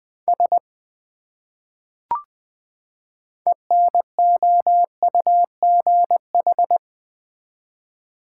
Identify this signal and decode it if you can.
Morse code, heard as a keyed tone, about 21 wpm, 700 Hz. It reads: S ENOUGH
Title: Build A Phrase - 2-Word Phrases - 20wpm